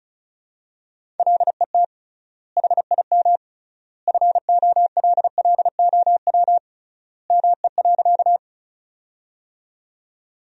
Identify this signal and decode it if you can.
Morse code, heard as a keyed tone, about 36 wpm, 700 Hz. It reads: LET HIM FOLLOW ME.
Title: 3x Word-Spacing - Sentences from Top 300 Words - 35wpm